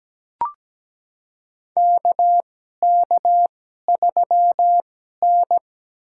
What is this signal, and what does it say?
Morse code, heard as a keyed tone, about 17 wpm, spaced as for 12 wpm, 700 Hz. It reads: KK3N
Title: USA Call Signs - 4 Characters - 2x Character Spacing - 17wpm